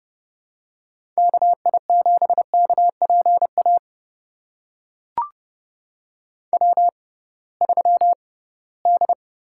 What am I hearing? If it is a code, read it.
Morse code, heard as a keyed tone, about 30 wpm, spaced as for 19 wpm, 700 Hz. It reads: KI7KPA W3D
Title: USA Call Signs - 6x Character Spacing - 30wpm